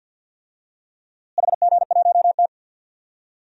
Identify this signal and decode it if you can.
Morse code, heard as a keyed tone, about 47 wpm, 700 Hz. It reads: HG1T
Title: DXCC-100 Call Signs - 4 Characters - Repeated 3x - 50wpm